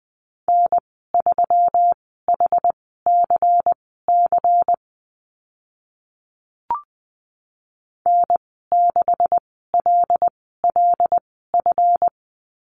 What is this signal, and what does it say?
Morse code, heard as a keyed tone, about 20 wpm, spaced as for 15 wpm, 700 Hz. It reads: N3HCC N6LLF
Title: 161 Lesson 26g - Letter g - Call Signs 20wpm